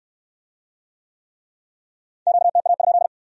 Morse code, heard as a keyed tone, about 66 wpm, 700 Hz. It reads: QTA?